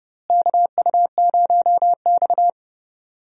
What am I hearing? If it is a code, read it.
Morse code, heard as a keyed tone, about 31 wpm, 700 Hz. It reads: KU0X
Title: USA Call Signs - 4 Characters - Repeated 2x - 30wpm